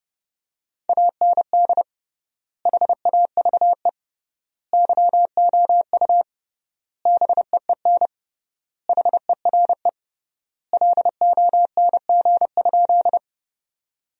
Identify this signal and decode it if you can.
Morse code, heard as a keyed tone, about 32 wpm, 700 Hz. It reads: AND HAVE YOU BEEN HERE LONG?
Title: 3x Word-Spacing - Sentences from Top 200 Words - 30wpm